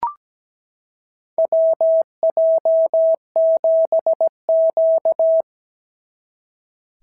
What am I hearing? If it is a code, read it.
Morse code, heard as a keyed tone, about 17 wpm, 650 Hz. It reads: WJ7Q
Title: SOTA QSO Practice 17wpm